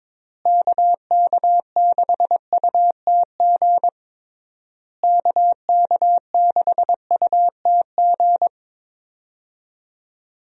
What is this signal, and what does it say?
Morse code, heard as a keyed tone, about 22 wpm, 700 Hz. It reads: KK6UTG KK6UTG
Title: USA Call Signs - 6 Characters - Repeated 3x - 22wpm